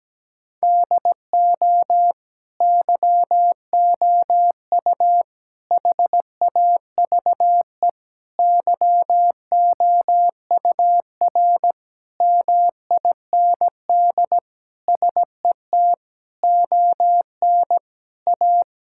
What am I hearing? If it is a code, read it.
Morse code, heard as a keyed tone, about 17 wpm, 700 Hz. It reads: DO YOU HAVE YOUR MIND SET ON A
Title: Sentences from Top 1000 Words - Review Part 5 - 17wpm